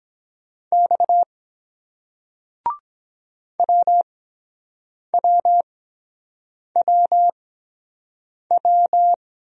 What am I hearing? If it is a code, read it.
Morse code, heard as a keyed tone, about 23 wpm, spaced as for 7 wpm, 700 Hz. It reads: X WWWW